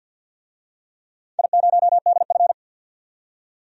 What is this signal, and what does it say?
Morse code, heard as a keyed tone, about 51 wpm, 700 Hz. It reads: I0DF
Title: DXCC-100 Call Signs - 4 Characters - 50wpm